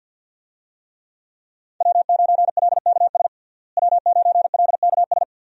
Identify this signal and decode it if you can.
Morse code, heard as a keyed tone, about 47 wpm, 700 Hz. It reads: W9LKS W9LKS
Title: USA Call Signs - 5 Characters - Repeated 2x - 50wpm